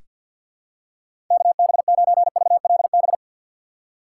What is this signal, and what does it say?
Morse code, heard as a keyed tone, about 46 wpm, 700 Hz. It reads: KB9VLB
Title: USA Call Signs - 6 Characters - 3x Character Spacing - 50wpm